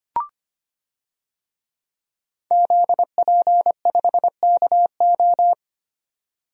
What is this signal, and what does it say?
Morse code, heard as a keyed tone, about 25 wpm, 700 Hz. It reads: ZP5KO
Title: DXCC-100 Call Signs - 5 Characters - Repeated 3x - 25wpm